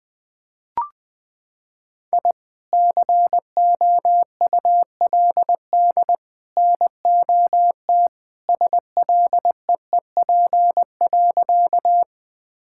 Morse code, heard as a keyed tone, about 20 wpm, 700 Hz. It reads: I COULD NOT SLEEP.
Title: Sentences with 3-Word Phrases - 20wpm